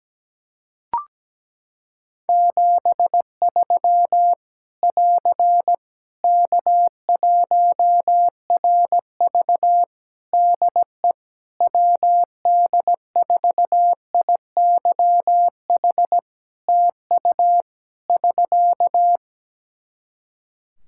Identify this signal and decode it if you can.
Morse code, heard as a keyed tone, about 17 wpm, 700 Hz. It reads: 73 <AR> K1RV DE WD4IYH TU <SK>
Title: Practice QSO - Exchange 4 - Ending - 17wpm